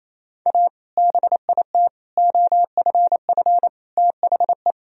Morse code, heard as a keyed tone, about 28 wpm, 700 Hz. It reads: A BIT OFF THE
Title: Sentences from Top 1000 Words - Review Part 2 - 28wpm